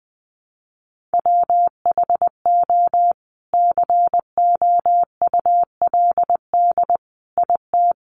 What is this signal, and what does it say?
Morse code, heard as a keyed tone, about 20 wpm, 700 Hz. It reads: WHO COULD IT
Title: Sentences from Top 1300 Words 20wpm